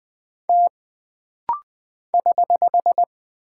Morse code, heard as a keyed tone, about 20 wpm, 700 Hz. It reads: T <HH>
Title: Single Character (Letter:Number:Punctuation:Prosign) - Rapid-Fire - 20wpm